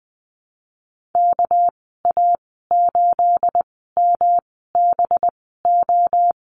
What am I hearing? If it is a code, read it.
Morse code, heard as a keyed tone, about 20 wpm, spaced as for 15 wpm, 700 Hz. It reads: KA8MBO
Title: 238 Lesson 38g - Letter q - Call Signs 20wpm